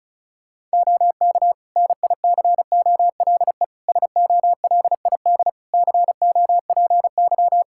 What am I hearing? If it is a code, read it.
Morse code, heard as a keyed tone, about 36 wpm, 700 Hz. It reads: OK NICOLE SOLID COPY
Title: Practice QSO - Exchange 3 - More Info - 35wpm